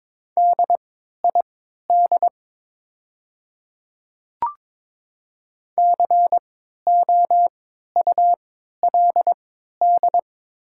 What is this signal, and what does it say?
Morse code, heard as a keyed tone, about 23 wpm, spaced as for 13 wpm, 700 Hz. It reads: DID COULD
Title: 3x Character Spacing - Top 100 Words - 22wpm